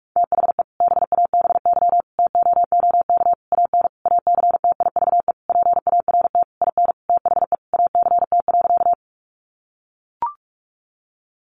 Morse code, heard as a keyed tone, about 44 wpm, 700 Hz. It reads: THE BABY TOOK AN ACTIVE PART IN THE ACT.
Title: Sentences from Top 1300 Words 45wpm